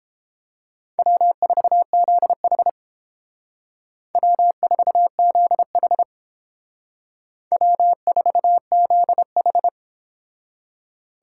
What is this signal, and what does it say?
Morse code, heard as a keyed tone, about 29 wpm, 700 Hz. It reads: W4ZH W4ZH W4ZH